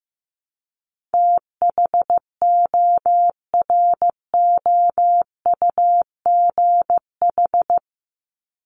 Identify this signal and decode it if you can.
Morse code, heard as a keyed tone, about 15 wpm, 700 Hz. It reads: THOROUGH
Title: LY Suffix - Top 100 Words with Base Words 15wpm